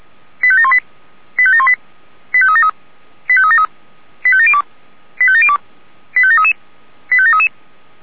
Five-tone selective call (ZVEI), heard as quick runs of five tones, tones of about 70 ms. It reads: 85718 85718 85373 85373 85702 85702 85733 85733